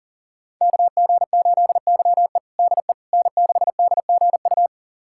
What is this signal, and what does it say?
Morse code, heard as a keyed tone, about 40 wpm, 700 Hz. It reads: KG8YE DE N6DGU
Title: USA Callsign DE Callsign - 5 Characters - 40wpm